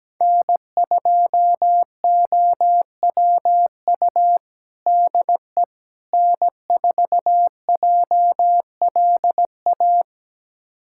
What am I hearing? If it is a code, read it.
Morse code, heard as a keyed tone, about 17 wpm, 700 Hz. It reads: N2OWU DE N4JLA
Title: USA Callsign DE Callsign - 5 Characters - 17wpm